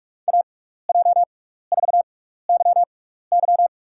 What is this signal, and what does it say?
Morse code, heard as a keyed tone, about 46 wpm, spaced as for 16 wpm, 700 Hz. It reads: AJ4YY
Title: USA Call Signs - 6x Character Spacing - 45wpm